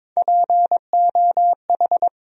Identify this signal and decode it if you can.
Morse code, heard as a keyed tone, about 22 wpm, 700 Hz. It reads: POH
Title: USA Call Sign Postfixes - 3 Letters 22wpm